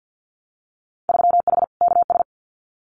Morse code, heard as a keyed tone, about 50 wpm, 700 Hz. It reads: 3H KS
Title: Winter Field Day Exchanges Encore 50wpm